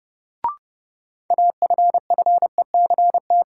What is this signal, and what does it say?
Morse code, heard as a keyed tone, about 30 wpm, 700 Hz. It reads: AFFECT
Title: Sentences from Top 900 Words - New Word Drill - 30wpm